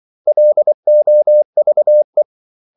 Morse code, heard as a keyed tone, about 24 wpm, 600 Hz. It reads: LOVE
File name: CW / Practice / 500MostCommonEnglishWords24